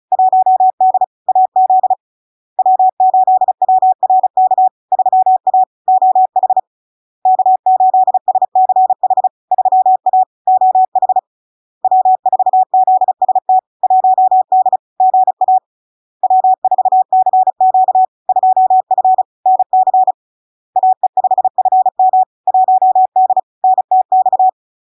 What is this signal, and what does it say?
Morse code, heard as a keyed tone, about 36 wpm, 750 Hz. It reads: 1D AZ W8WRK 3A OH K8SCH 3A OH W4ZST 1D GA W4CQ 2F NC AE5FM 1D NTX
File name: LARK-FD-2022-Log-in-CW-35wpm-750-Hz